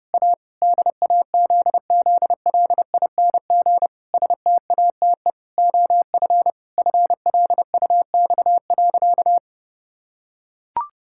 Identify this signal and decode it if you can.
Morse code, heard as a keyed tone, about 32 wpm, 700 Hz. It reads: A DAZZLING STATE OF FLUX.
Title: Pangram Challenge v2 - 30wpm